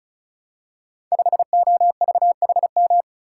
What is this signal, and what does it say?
Morse code, heard as a keyed tone, about 36 wpm, 700 Hz. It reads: 5OVHM
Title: Code Groups - 5 Letter:Numbers - 35wpm